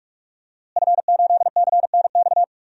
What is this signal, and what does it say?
Morse code, heard as a keyed tone, about 46 wpm, 700 Hz. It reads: F8CNX
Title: DXCC-100 Call Signs - 5 Characters - Repeated 2x - 45wpm